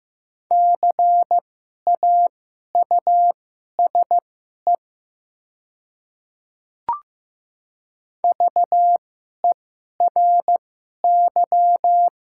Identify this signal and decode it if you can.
Morse code, heard as a keyed tone, about 15 wpm, spaced as for 11 wpm, 700 Hz. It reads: CAUSE VERY
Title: 2x Character Spacing - Top 300 Words - 15wpm